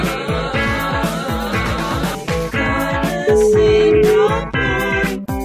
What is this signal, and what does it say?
Ringback tone (Europe, 425 Hz), with noise.